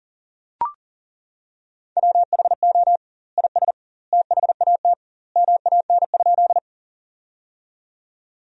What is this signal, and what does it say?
Morse code, heard as a keyed tone, about 40 wpm, 700 Hz. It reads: WHO IS THAT MAN?
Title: 2x Word-Spacing - Sentences from Top 200 Words - 40wpm